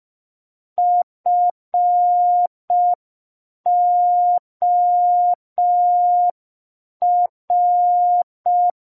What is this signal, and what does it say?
Morse code, heard as a keyed tone, about 5 wpm, 710 Hz. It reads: FOR